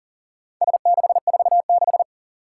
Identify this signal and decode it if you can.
Morse code, heard as a keyed tone, about 38 wpm, 700 Hz. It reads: S646
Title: Four Challenging Letter-Number Combinations 40wpm